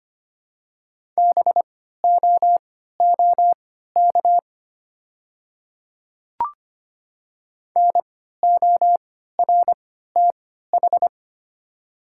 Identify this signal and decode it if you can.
Morse code, heard as a keyed tone, about 25 wpm, spaced as for 14 wpm, 700 Hz. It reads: BOOK NORTH
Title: 3x Character Spacing - Top 500 Words - 25wpm